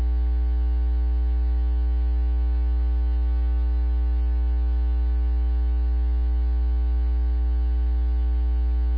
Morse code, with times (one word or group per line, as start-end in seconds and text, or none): none